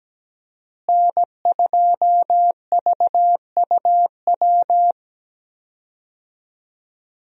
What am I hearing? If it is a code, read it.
Morse code, heard as a keyed tone, about 17 wpm, 700 Hz. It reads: N2VUW
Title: USA Call Signs - 5 Characters - Repeated 3x - 17wpm